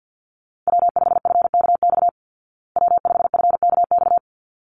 Morse code, heard as a keyed tone, about 50 wpm, 700 Hz. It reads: W5FKX W5FKX
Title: Call Signs Repeated 3x - 50wpm